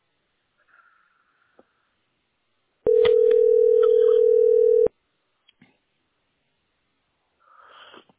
Ringback tone (North America), 440 + 480 Hz.